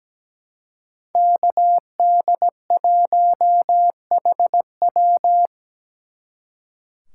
Morse code, heard as a keyed tone, about 20 wpm, 700 Hz. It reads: KD1HW
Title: Speed Racing - USA Call Signs 17wpm